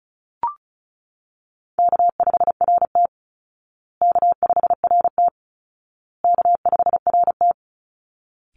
Morse code, heard as a keyed tone, about 34 wpm, 700 Hz. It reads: K5RT K5RT K5RT
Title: Easy Call Signs Repeated 3x - 35wpm